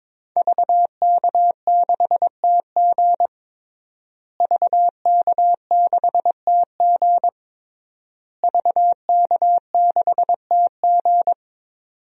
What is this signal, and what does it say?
Morse code, heard as a keyed tone, about 22 wpm, 700 Hz. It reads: VK6TG VK6TG VK6TG
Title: DXCC-100 Call Signs - 5 Characters - Repeated 3x - 22wpm